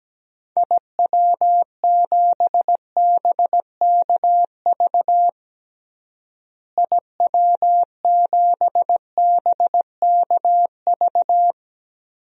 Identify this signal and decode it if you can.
Morse code, heard as a keyed tone, about 17 wpm, 700 Hz. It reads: IW7BKV IW7BKV
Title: DXCC-100 Call Signs - 6 Characters - Repeated 3x - 17wpm